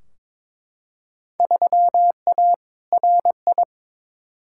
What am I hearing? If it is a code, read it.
Morse code, heard as a keyed tone, about 22 wpm, 700 Hz. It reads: 3A RI
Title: Field Day Exchanges - 22wpm